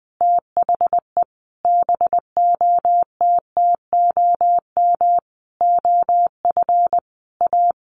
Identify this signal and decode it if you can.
Morse code, heard as a keyed tone, about 20 wpm, 700 Hz. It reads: THE BOTTOM OF A
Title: Sentences from Top 900 Words 20wpm